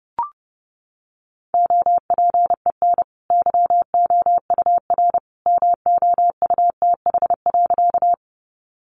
Morse code, heard as a keyed tone, about 30 wpm, 700 Hz. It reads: OPEN YOUR MOUTH.
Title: Sentences from Top 1500 Words 30wpm